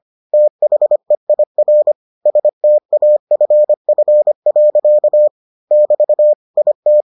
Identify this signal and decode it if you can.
Morse code, heard as a keyed tone, about 25 wpm, 600 Hz. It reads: THEIR STAFF. <BT> IT